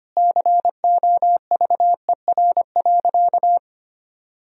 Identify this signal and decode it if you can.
Morse code, heard as a keyed tone, about 25 wpm, 700 Hz. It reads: COVER.
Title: Sentences from Top 1000 Words - Review Part 1 - 25wpm